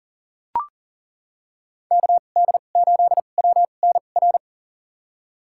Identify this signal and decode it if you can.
Morse code, heard as a keyed tone, about 39 wpm, spaced as for 28 wpm, 700 Hz. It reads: KD8WNR
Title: USA Call Signs - 6 Characters - 2x Character Spacing - 40wpm